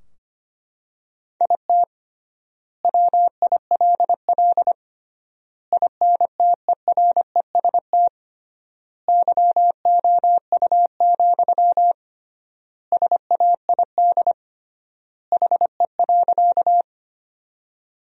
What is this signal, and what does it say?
Morse code, heard as a keyed tone, about 25 wpm, 700 Hz. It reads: IT WILL INTEREST YOU, SAID HE.
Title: 3x Word-Spacing - Sentences from Top 400 Words - 25wpm